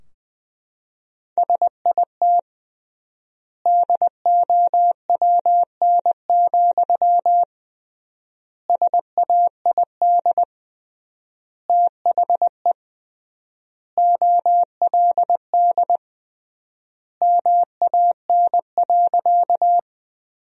Morse code, heard as a keyed tone, about 20 wpm, 700 Hz. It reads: SIT DOWN, SAID THE OLD MAN.
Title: 3x Word-Spacing - Sentences from Top 500 Words - 20wpm